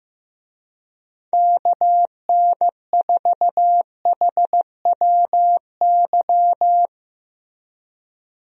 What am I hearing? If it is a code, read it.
Morse code, heard as a keyed tone, about 15 wpm, 700 Hz. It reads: KN4HWY
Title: USA Call Signs - 6 Characters - Repeated 3x - 15wpm